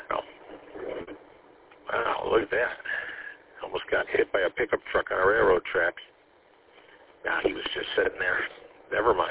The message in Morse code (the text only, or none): none